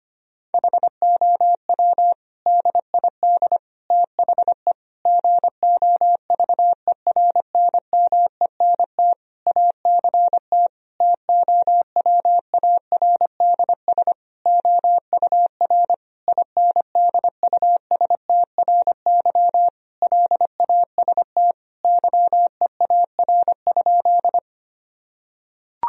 Morse code, heard as a keyed tone, about 25 wpm, 700 Hz. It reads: HOW DID THE GOVERNMENT ACT TOWARDS OUR INDUSTRY LAST YEAR?
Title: Sentences from Top 1000 Words - Review Part 4 - 25wpm